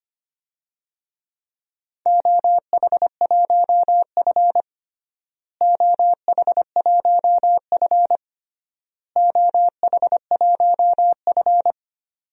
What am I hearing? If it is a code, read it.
Morse code, heard as a keyed tone, about 25 wpm, 700 Hz. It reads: OH1F OH1F OH1F
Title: DXCC-100 Call Signs - 4 Characters - Repeated 3x - 25wpm